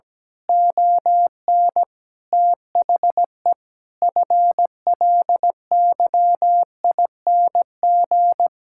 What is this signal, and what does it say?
Morse code, heard as a keyed tone, about 17 wpm, 700 Hz. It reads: ON THE FLYING
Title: Pangram Challenge v2 - 17wpm